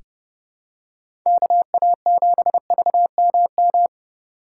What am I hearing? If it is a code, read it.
Morse code, heard as a keyed tone, about 30 wpm, 700 Hz. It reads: KA7VMM
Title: USA Call Signs - 8x Character Spacing - 30wpm